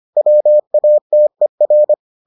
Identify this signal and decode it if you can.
Morse code, heard as a keyed tone, about 25 wpm, 600 Hz. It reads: WATER